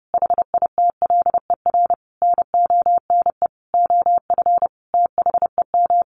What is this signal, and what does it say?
Morse code, heard as a keyed tone, about 28 wpm, 700 Hz. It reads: HITLER NONE OF THEM
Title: A Sound Of Thunder 30wpm